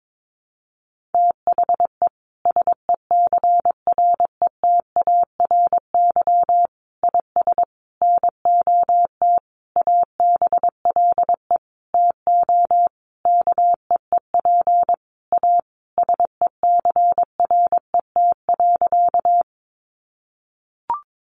Morse code, heard as a keyed tone, about 22 wpm, 700 Hz. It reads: THE SECRETARY IS NOT ABLE TO KEEP A SECRET.
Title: Sentences from Top 1300 Words 22wpm